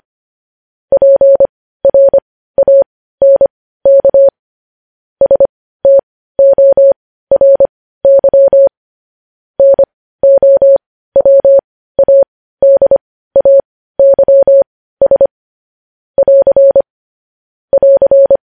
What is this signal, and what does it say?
Morse code, heard as a keyed tone, about 25 wpm, spaced as for 15 wpm, 550 Hz. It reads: PRANK STORY NOWADAYS <AR> <AR>
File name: podcast / News headlines-1x-25x15